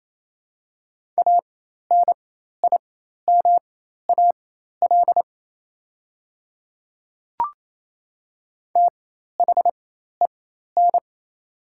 Morse code, heard as a keyed tone, about 28 wpm, spaced as for 13 wpm, 700 Hz. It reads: ANIMAL THEN
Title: 4x Character Spacing - Top 200 Words - 28wpm